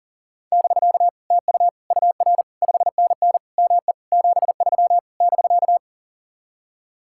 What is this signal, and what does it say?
Morse code, heard as a keyed tone, about 40 wpm, 700 Hz. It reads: <BK> TU UR 5NN ME 73 <BK>
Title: Speed-Racing Common QSO Phrases - 40wpm